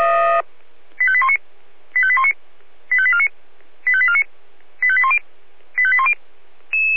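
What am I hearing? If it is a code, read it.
Five-tone selective call (ZVEI), heard as quick runs of five tones, tones of about 70 ms. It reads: 85819 85819 85849 85849 85810 85810
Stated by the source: Five-tone ZVEI selective call received on SDR.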